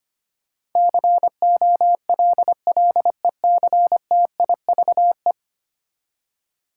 Morse code, heard as a keyed tone, about 25 wpm, 700 Hz. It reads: COLLECTIVE
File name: Ten Letter Words - 25wpm